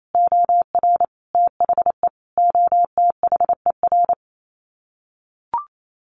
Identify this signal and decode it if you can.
Morse code, heard as a keyed tone, about 28 wpm, 700 Hz. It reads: OR THE OTHER
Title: Sets of 5 Words 28wpm